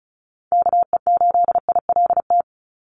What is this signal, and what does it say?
Morse code, heard as a keyed tone, about 36 wpm, 700 Hz. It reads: KE8ILT
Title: USA Call Signs 35wpm